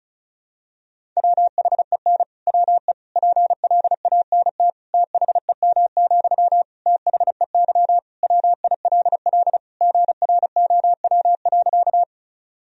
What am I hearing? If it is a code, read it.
Morse code, heard as a keyed tone, about 36 wpm, 700 Hz. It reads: WHEN WE PLANT THEM, THEY WILL GROW.
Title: Sentences from Top 1000 Words - Review Part 1 - 35wpm